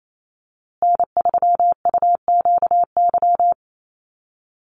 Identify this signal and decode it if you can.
Morse code, heard as a keyed tone, about 28 wpm, 700 Hz. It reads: N3UQY
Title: USA Call Signs Repeated 3x 28wpm